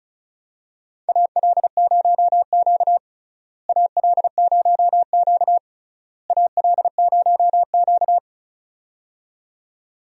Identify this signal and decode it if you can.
Morse code, heard as a keyed tone, about 36 wpm, 700 Hz. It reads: AL0Q AL0Q AL0Q
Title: USA Call Signs - 4 Characters - Repeated 3x - 35wpm